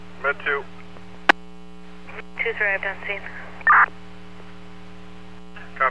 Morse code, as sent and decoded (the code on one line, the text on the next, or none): none